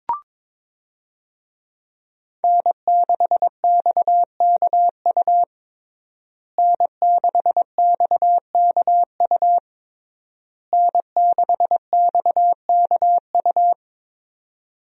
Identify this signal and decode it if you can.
Morse code, heard as a keyed tone, about 22 wpm, 700 Hz. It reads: N6XKU N6XKU N6XKU
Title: USA Call Signs - 5 Characters - Repeated 3x - 22wpm